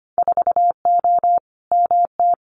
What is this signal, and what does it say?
Morse code, heard as a keyed tone, about 25 wpm, 700 Hz. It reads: 4O MT
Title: Winter Field Day Exchanges Encore 25wpm